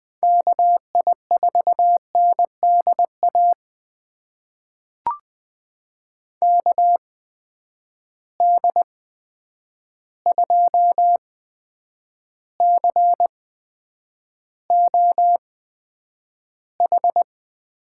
Morse code, heard as a keyed tone, about 20 wpm, spaced as for 9 wpm, 700 Hz. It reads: KI4NDA KD2COH
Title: USA Call Signs - 6 Characters - 8x Character Spacing - 20wpm